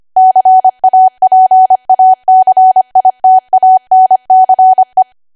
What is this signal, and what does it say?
Morse code, heard as a keyed tone, about 25 wpm, 750 Hz. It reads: CAPACITANCE